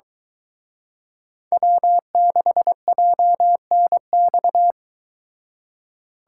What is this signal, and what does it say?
Morse code, heard as a keyed tone, about 23 wpm, 700 Hz. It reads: W6JNX